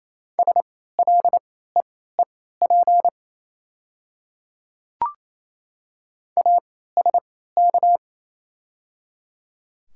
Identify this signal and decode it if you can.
Morse code, heard as a keyed tone, about 28 wpm, spaced as for 16 wpm, 700 Hz. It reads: SLEEP ASK
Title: 3x Character Spacing - Top 500 Words - 28wpm